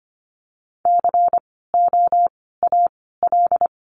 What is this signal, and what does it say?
Morse code, heard as a keyed tone, about 25 wpm, 700 Hz. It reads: C O A L
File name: course / 25:10wpm / 094 Lesson 16f - Letter c - Words 25wpm